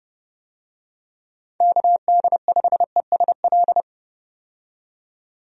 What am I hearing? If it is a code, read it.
Morse code, heard as a keyed tone, about 30 wpm, 700 Hz. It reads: KD5ESL
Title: USA Call Signs - 6 Characters - 30wpm